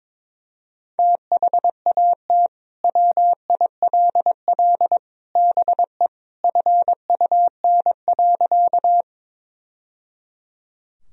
Morse code, heard as a keyed tone, about 22 wpm, 700 Hz. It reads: THAT WILL BE FUN.
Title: Sentences from Top 1000 Words - Review Part 4 - 22wpm